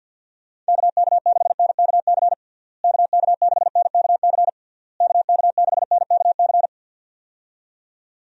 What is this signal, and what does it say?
Morse code, heard as a keyed tone, about 52 wpm, 700 Hz. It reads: KK6NKC KK6NKC KK6NKC
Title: USA Call Signs - 6 Characters - Repeated 3x - 50wpm